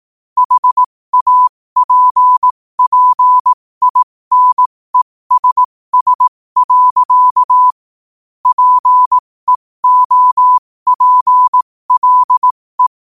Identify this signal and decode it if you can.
Morse code, heard as a keyed tone, about 18 wpm, spaced as for 15 wpm, 1000 Hz. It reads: HAPPINESS. PEOPLE
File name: qotdfiles / 15wpm